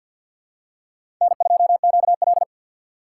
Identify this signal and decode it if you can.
Morse code, heard as a keyed tone, about 48 wpm, 700 Hz. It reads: N2QL
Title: USA Call Signs - 4x Character Spacing - 50wpm